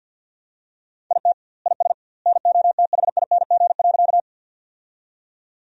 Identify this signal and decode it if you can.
Morse code, heard as a keyed tone, about 50 wpm, 700 Hz. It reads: IT IS NOTHING.
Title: 2x Word-Spacing - Sentences from Top 500 Words - 50wpm